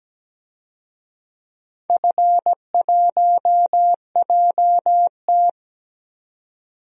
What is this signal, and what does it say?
Morse code, heard as a keyed tone, about 17 wpm, 700 Hz. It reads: F1JT
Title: DXCC-100 Call Signs - 4 Characters - 17wpm